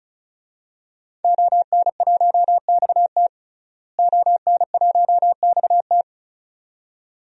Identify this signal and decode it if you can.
Morse code, heard as a keyed tone, about 34 wpm, 700 Hz. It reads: ON1XT ON1XT
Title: DXCC-100 Call Signs - 5 Characters - Repeated 2x - 35wpm